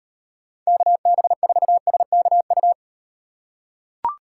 Morse code, heard as a keyed tone, about 38 wpm, 700 Hz. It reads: KB4SKU